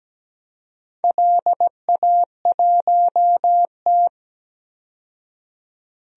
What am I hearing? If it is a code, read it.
Morse code, heard as a keyed tone, about 17 wpm, 700 Hz. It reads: LA1T
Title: DXCC-100 Call Signs - 4 Characters - 17wpm